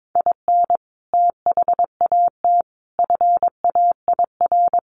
Morse code, heard as a keyed tone, about 22 wpm, 700 Hz. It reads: IN THAT FAIR
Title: Sentences from Top 1500 Words 22wpm